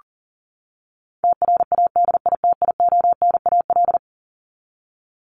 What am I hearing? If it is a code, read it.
Morse code, heard as a keyed tone, about 37 wpm, 700 Hz. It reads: TRADITIONAL
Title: Top 751-1000 Words 40wpm